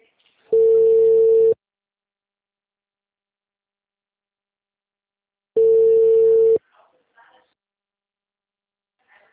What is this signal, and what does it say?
Ringback tone (China, 450 Hz).